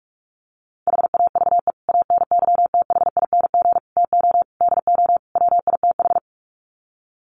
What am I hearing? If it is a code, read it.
Morse code, heard as a keyed tone, about 44 wpm, 700 Hz. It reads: HAVE ANYTHING TO DO WITH
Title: Sets of 5 Words 45wpm